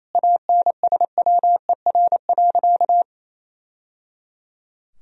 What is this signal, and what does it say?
Morse code, heard as a keyed tone, about 28 wpm, 700 Hz. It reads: ANSWER.
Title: Sentences from Top 1000 Words - Review Part 2 - 28wpm